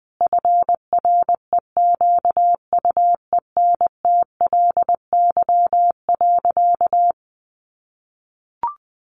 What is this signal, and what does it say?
Morse code, heard as a keyed tone, about 20 wpm, 700 Hz. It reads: FREQUENTLY.
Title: Sentences from Top 1300 Words 20wpm